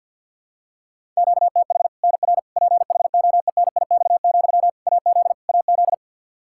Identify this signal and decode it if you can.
Morse code, heard as a keyed tone, about 51 wpm, 700 Hz. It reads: QTH NR PHOENIX, AZ AZ
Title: K1USN SST QSO Practice 50wpm